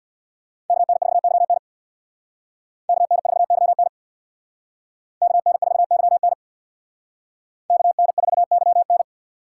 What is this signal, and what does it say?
Morse code, heard as a keyed tone, about 64 wpm, 700 Hz. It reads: KN4YN KN4YN KN4YN KN4YN